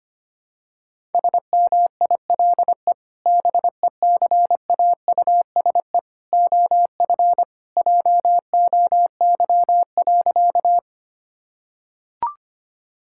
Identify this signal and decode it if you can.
Morse code, heard as a keyed tone, about 25 wpm, 700 Hz. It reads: SMILE BECAUSE OF JOY.
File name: practice / 25wpm / Sentences with 2-Word Phrases - 25wpm